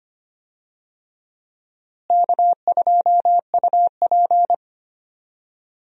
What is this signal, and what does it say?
Morse code, heard as a keyed tone, about 25 wpm, 700 Hz. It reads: K2UP
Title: USA Call Signs - 4 Characters - Repeated 3x - 25wpm